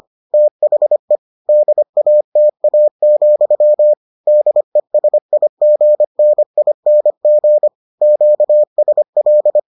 Morse code, heard as a keyed tone, about 25 wpm, 600 Hz. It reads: THE DATA, DESIGNING QSL